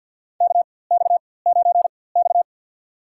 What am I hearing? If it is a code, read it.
Morse code, heard as a keyed tone, about 52 wpm, spaced as for 23 wpm, 700 Hz. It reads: KX9X
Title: USA Call Signs - 4x Character Spacing - 50wpm